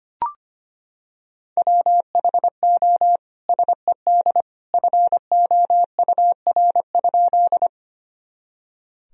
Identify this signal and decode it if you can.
Morse code, heard as a keyed tone, about 25 wpm, 700 Hz. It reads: WHO SED FOUR?
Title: Sentences from Top 300 Words and Abbreviations 25wpm